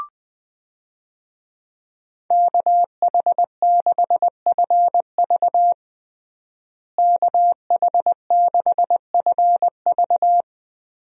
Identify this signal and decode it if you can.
Morse code, heard as a keyed tone, about 20 wpm, 700 Hz. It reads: KH6FV KH6FV
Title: USA Call Signs - 5 Characters - Repeated 2x - 20wpm